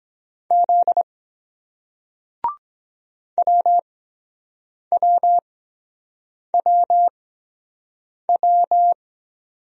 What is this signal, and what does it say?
Morse code, heard as a keyed tone, about 23 wpm, spaced as for 7 wpm, 700 Hz. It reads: Z WWWW